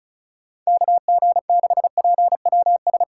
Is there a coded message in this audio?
Yes: Morse code from 0.7 to 3.0 s, heard as a keyed tone, about 45 wpm, 700 Hz. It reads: KG6PWS